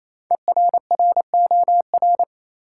Morse code, heard as a keyed tone, about 28 wpm, 700 Hz. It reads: ERROR
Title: Five Letter Words - 28wpm